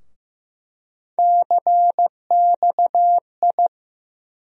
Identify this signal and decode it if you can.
Morse code, heard as a keyed tone, about 15 wpm, 700 Hz. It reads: CXI
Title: USA Call Sign Postfixes - 3 Letters 15wpm